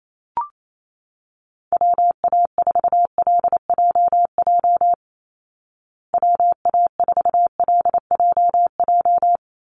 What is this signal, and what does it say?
Morse code, heard as a keyed tone, about 28 wpm, 700 Hz. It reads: WA4LJJ WA4LJJ
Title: Call Signs Repeated 3x - 28wpm